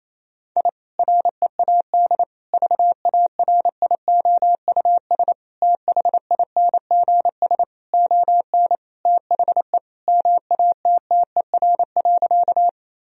Morse code, heard as a keyed tone, about 28 wpm, 700 Hz. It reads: I READ VARIOUS THINGS ON THE MATTER.
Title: Sentences from Top 1000 Words - Review Part 4 - 28wpm